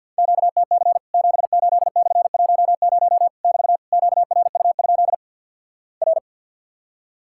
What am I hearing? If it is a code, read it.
Morse code, heard as a keyed tone, about 50 wpm, 700 Hz. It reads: QTC 78/10 <BT> QRU? R
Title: Worked All Europe DX Contest - QTC Exchange with Repeats - 50wpm